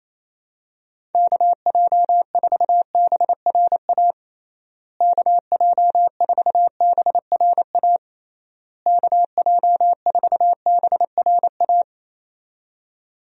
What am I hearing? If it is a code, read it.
Morse code, heard as a keyed tone, about 27 wpm, 700 Hz. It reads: KJ4BRA KJ4BRA KJ4BRA
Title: USA Call Signs - 6 Characters - Repeated 3x - 28wpm